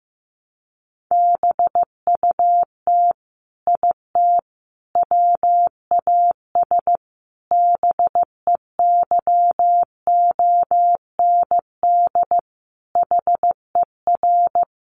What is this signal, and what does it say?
Morse code, heard as a keyed tone, about 15 wpm, 700 Hz. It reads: BUT IT WAS BEYOND HER
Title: Sentences from Top 1200 Words 15wpm